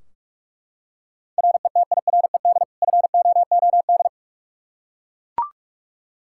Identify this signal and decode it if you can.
Morse code, heard as a keyed tone, about 41 wpm, 700 Hz. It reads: RETIRED FOOD
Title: K1USN SST QSO Practice 45wpm